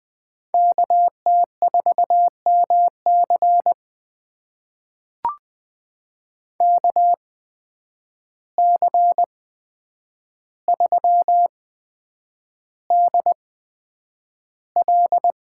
Morse code, heard as a keyed tone, about 20 wpm, spaced as for 9 wpm, 700 Hz. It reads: KT4MC KC3DL
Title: USA Call Signs - 8x Character Spacing - 20wpm